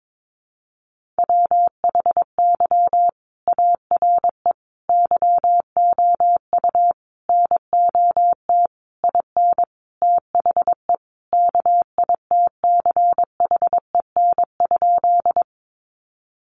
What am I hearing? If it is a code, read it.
Morse code, heard as a keyed tone, about 22 wpm, 700 Hz. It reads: WHY ARE YOU NOT IN THE KITCHEN?
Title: Sentences from Top 1400 Words 22wpm